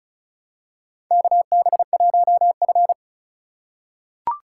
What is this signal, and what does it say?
Morse code, heard as a keyed tone, about 34 wpm, 700 Hz. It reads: KB1F
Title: USA Call Signs - 4 Characters - 3x Character Spacing - 35wpm